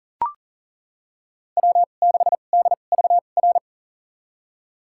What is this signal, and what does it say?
Morse code, heard as a keyed tone, about 40 wpm, spaced as for 29 wpm, 700 Hz. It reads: W6DVR
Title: USA Call Signs - 5 Characters - 2x Character Spacing - 40wpm